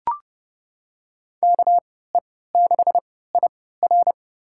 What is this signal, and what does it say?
Morse code, heard as a keyed tone, about 30 wpm, spaced as for 17 wpm, 700 Hz. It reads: KE6IR
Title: USA Call Signs - 5 Characters - 3x Character Spacing - 30wpm